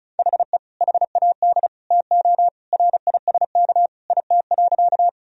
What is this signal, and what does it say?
Morse code, heard as a keyed tone, about 34 wpm, 700 Hz. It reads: HE HAD TO RISK IT.
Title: Sentences from Top 1000 Words - Review Part 4 - 35wpm